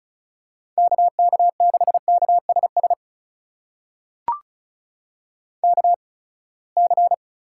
Morse code, heard as a keyed tone, about 34 wpm, spaced as for 24 wpm, 700 Hz. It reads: KK6KSS KC
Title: USA Call Signs - 6 Characters - 8x Character Spacing - 35wpm